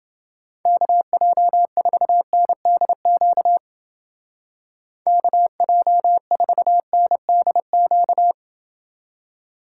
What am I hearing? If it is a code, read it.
Morse code, heard as a keyed tone, about 28 wpm, 700 Hz. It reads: KJ4NDQ KJ4NDQ